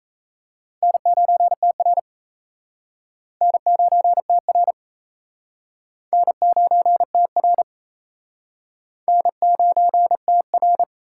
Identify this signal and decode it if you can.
Morse code, heard as a keyed tone, about 34 wpm, 700 Hz. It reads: N9TR N9TR N9TR N9TR
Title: Speed Racing - USA 1x2 Call Signs 28wpm